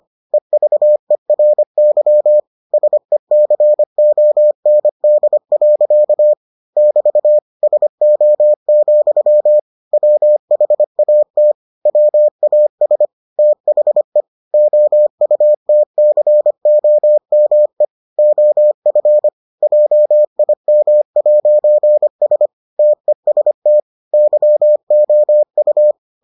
Morse code, heard as a keyed tone, about 25 wpm, 600 Hz. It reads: EVERY SECOND. <BT> SO, WHAT WAS THE OUTCOME OF JIM'S TEST YOU